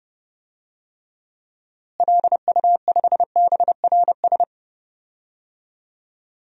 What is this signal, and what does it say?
Morse code, heard as a keyed tone, about 30 wpm, 700 Hz. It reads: LU5BRS
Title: DXCC-100 Call Signs - 6 Characters - Repeated 3x - 30wpm